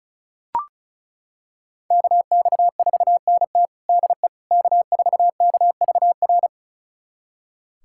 Morse code, heard as a keyed tone, about 36 wpm, 700 Hz. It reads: KX4NT DE K4KVR
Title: USA Callsign DE Callsign - 5 Characters - 35wpm